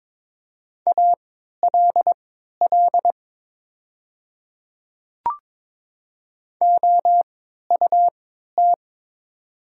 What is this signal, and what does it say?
Morse code, heard as a keyed tone, about 22 wpm, spaced as for 13 wpm, 700 Hz. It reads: ALL OUT
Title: 3x Character Spacing - Top 100 Words - 22wpm